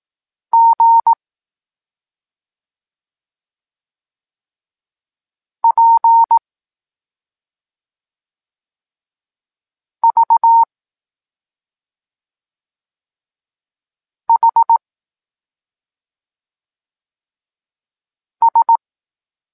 Morse code, heard as a keyed tone, about 18 wpm, 920 Hz. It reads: G P V H S